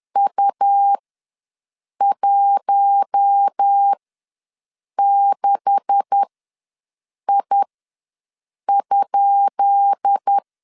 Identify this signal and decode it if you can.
Morse code, heard as a keyed tone, about 10 wpm, spaced as for 6 wpm, 800 Hz. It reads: U16I?
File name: LEZIONI / 10 SIGNS / 10D COMA ws 10wpm 4314